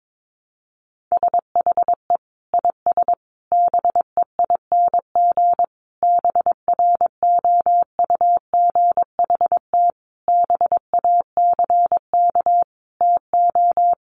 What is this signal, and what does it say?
Morse code, heard as a keyed tone, about 22 wpm, 700 Hz. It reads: SHE IS BEING BROUGHT BACK TO
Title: Sentences from Top 600 Words 22wpm